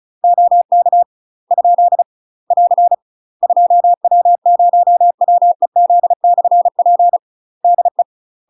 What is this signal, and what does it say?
Morse code, heard as a keyed tone, about 37 wpm, 700 Hz. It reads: OK ? <AR> 2W0WEZ/P DE
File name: UK MASTER QSOs TRAINING / 35WPM / UK MASTER QSOs 0043